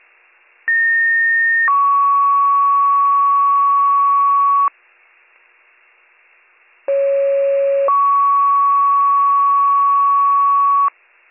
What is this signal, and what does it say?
Two-tone page, heard as a one-second tone, then a three-second tone, about 1850 then 1120 Hz, about 550 then 1120 Hz. It is heard from 0.7 to 10.9 s.